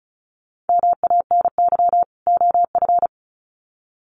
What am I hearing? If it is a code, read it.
Morse code, heard as a keyed tone, about 34 wpm, 700 Hz. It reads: MANY OF
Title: Sets of 2 Words 35wpm